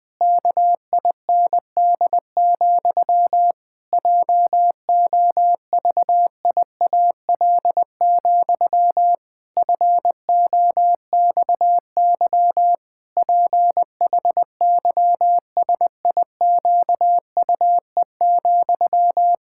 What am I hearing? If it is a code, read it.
Morse code, heard as a keyed tone, about 20 wpm, 700 Hz. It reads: KIND, JOVIAL, FOXY PHYSIQUE,
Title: Pangram Challenge v2 - 20wpm